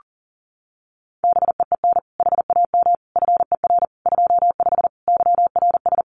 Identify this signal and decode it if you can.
Morse code, heard as a keyed tone, about 37 wpm, 700 Hz. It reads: BEEN HAM FER 25 YRS
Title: Common QSO Phrases Encore 40wpm